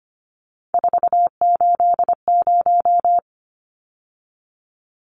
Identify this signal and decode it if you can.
Morse code, heard as a keyed tone, about 25 wpm, 700 Hz. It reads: 480
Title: Numbers - 3-Digit 25wpm